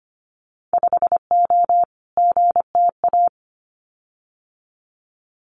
Morse code, heard as a keyed tone, about 25 wpm, 700 Hz. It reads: 5O GTA
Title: Winter Field Day Exchanges Encore 25wpm